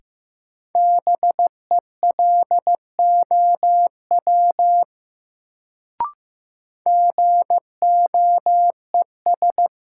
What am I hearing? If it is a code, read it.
Morse code, heard as a keyed tone, about 15 wpm, 700 Hz. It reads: BELOW GOES
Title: Sentences from Top 1000 Words - Review Part 5 - New Word Drill - 15wpm